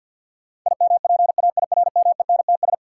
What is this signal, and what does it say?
Morse code, heard as a keyed tone, about 51 wpm, 700 Hz. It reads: IMPAIRMENTS
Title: Sets of 5 Words Encore - New Word Drill - 50wpm